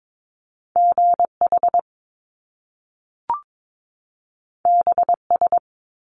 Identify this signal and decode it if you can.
Morse code, heard as a keyed tone, about 22 wpm, 700 Hz. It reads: GH BS
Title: Top Two Letter Combinations in Words 22wpm